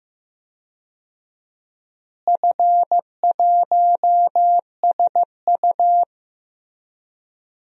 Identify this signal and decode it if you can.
Morse code, heard as a keyed tone, about 15 wpm, 700 Hz. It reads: F1SU
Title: DXCC-100 Call Signs - 4 Characters - Repeated 2x - 15wpm